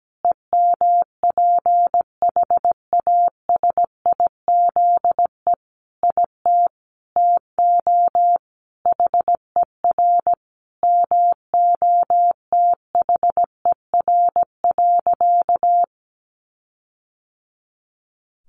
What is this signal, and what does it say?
Morse code, heard as a keyed tone, about 17 wpm, 700 Hz. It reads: EMPHASIZE IT TO HER MOTHER.
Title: Sentences from Top 1500 Words 17wpm